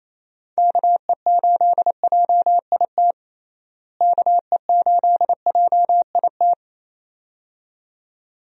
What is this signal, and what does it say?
Morse code, heard as a keyed tone, about 27 wpm, 700 Hz. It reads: KE8JIT KE8JIT
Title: USA Call Signs - 6 Characters - Repeated 2x - 28wpm